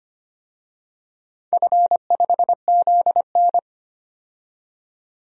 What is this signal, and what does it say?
Morse code, heard as a keyed tone, about 25 wpm, 700 Hz. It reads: F5ZN
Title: DXCC-100 Call Signs - 4 Characters - 25wpm